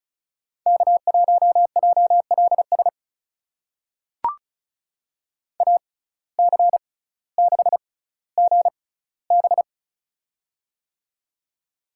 Morse code, heard as a keyed tone, about 36 wpm, spaced as for 18 wpm, 700 Hz. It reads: K1JLS AC6GB
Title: USA Call Signs - 5 Characters - 6x Character Spacing - 35wpm